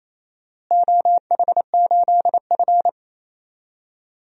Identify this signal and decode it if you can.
Morse code, heard as a keyed tone, about 27 wpm, 700 Hz. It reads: OH8F
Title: DXCC-100 Call Signs - 4 Characters - Repeated 3x - 28wpm